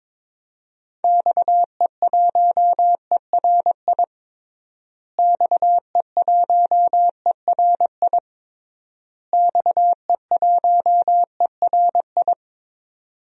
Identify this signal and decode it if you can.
Morse code, heard as a keyed tone, about 22 wpm, 700 Hz. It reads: XE1ERI XE1ERI XE1ERI
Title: DXCC-100 Call Signs - 6 Characters - Repeated 3x - 22wpm